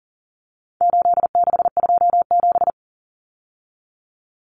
Morse code, heard as a keyed tone, about 40 wpm, 700 Hz. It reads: 8627
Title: Numbers - 4-Digit 40wpm